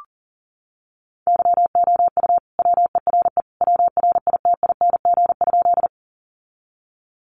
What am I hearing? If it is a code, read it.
Morse code, heard as a keyed tone, about 40 wpm, 700 Hz. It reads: YOU WERE WRITING?
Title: Sentences from Top 800 Words 40wpm